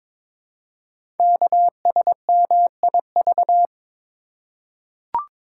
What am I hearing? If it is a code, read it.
Morse code, heard as a keyed tone, about 22 wpm, 700 Hz. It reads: KSMIV
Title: Code Groups - 5 Letters - 22wpm